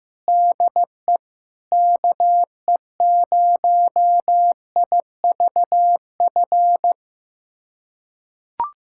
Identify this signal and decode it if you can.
Morse code, heard as a keyed tone, about 15 wpm, 700 Hz. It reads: DE KE0IVF
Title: USA Callsign DE Callsign - 6 Characters - 15wpm